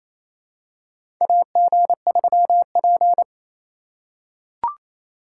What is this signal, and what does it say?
Morse code, heard as a keyed tone, about 28 wpm, 700 Hz. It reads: AG3P
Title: USA Call Signs - 4 Characters - 4x Character Spacing - 28wpm